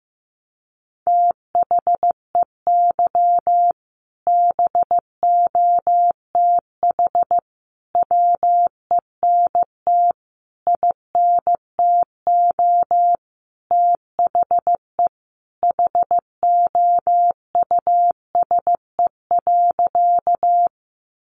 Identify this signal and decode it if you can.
Morse code, heard as a keyed tone, about 15 wpm, 700 Hz. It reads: THEY BOTH WENT INTO THE HOUSE.
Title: Sentences from Top 600 Words 15wpm